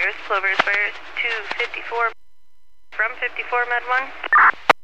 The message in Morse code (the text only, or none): none